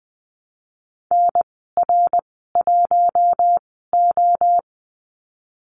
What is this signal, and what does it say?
Morse code, heard as a keyed tone, about 20 wpm, spaced as for 15 wpm, 700 Hz. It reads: NR1O
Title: 062 Lesson 11g - Letter h - Call Signs 20wpm